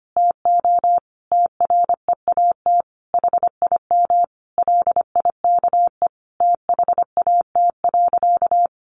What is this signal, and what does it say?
Morse code, heard as a keyed tone, about 25 wpm, 700 Hz. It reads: TO TREAT HIM LIKE THAT.
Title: Sentences from Top 1300 Words 25wpm